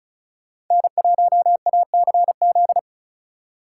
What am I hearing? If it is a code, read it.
Morse code, heard as a keyed tone, about 46 wpm, 700 Hz. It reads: N1ACZ